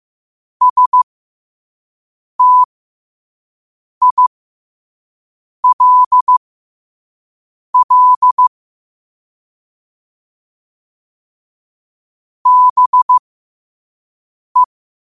Morse code, heard as a keyed tone, about 15 wpm, spaced as for 5 wpm, 1000 Hz. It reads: STILL BE
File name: 5wpm